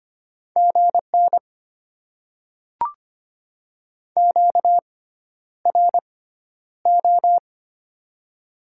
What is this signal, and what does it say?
Morse code, heard as a keyed tone, about 25 wpm, spaced as for 11 wpm, 700 Hz. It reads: GN QRO
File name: Top 90 QSO Elements - 6x Character Spacing - 25wpm